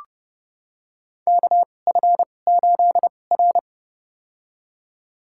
Morse code, heard as a keyed tone, about 32 wpm, spaced as for 22 wpm, 700 Hz. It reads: KF8R
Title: USA Call Signs - 4 Characters - 2x Character Spacing - 30wpm